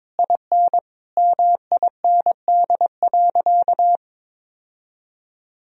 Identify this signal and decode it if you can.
Morse code, heard as a keyed tone, about 22 wpm, 700 Hz. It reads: IN MIND.
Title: Sentences from Top 1000 Words - Review Part 3 - 22wpm